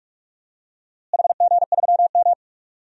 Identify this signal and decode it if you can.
Morse code, heard as a keyed tone, about 46 wpm, 700 Hz. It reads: HG3M
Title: DXCC-100 Call Signs - 4 Characters - Repeated 2x - 45wpm